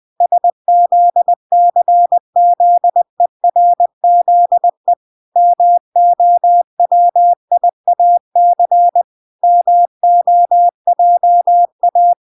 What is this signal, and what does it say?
Morse code, heard as a keyed tone, about 20 wpm, 700 Hz. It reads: SZCZERZE MOWIAC MOJA